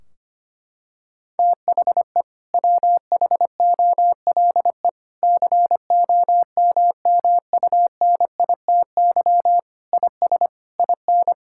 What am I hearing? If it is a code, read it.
Morse code, heard as a keyed tone, about 25 wpm, 700 Hz. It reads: THE WHOLE COMMUNITY IS IN
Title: Sentences from Top 1600 Words 25wpm